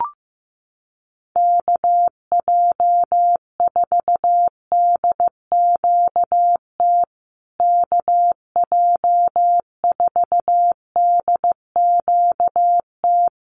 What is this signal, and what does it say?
Morse code, heard as a keyed tone, about 15 wpm, 700 Hz. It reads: KJ4DQT KJ4DQT
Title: USA Call Signs Repeated 3x 15wpm